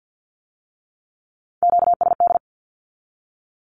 Morse code, heard as a keyed tone, about 50 wpm, 700 Hz. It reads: QSD
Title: Q-Codes with Spoken Meaning - 50wpm